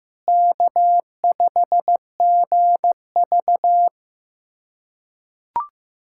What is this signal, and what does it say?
Morse code, heard as a keyed tone, about 15 wpm, 700 Hz. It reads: K5GV
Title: USA Callsign DE Callsign - 4 Characters - 15wpm